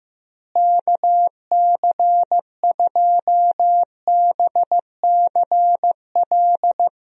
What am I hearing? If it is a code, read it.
Morse code, heard as a keyed tone, about 15 wpm, 700 Hz. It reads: KC2BCL
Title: USA Call Signs - 6 Characters - Repeated 2x - 15wpm